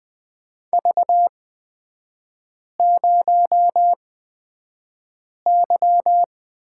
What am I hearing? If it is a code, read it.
Morse code, heard as a keyed tone, about 20 wpm, spaced as for 5 wpm, 700 Hz. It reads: V0Y
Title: 236 Lesson 38e - Letter q - Three Characters 20wpm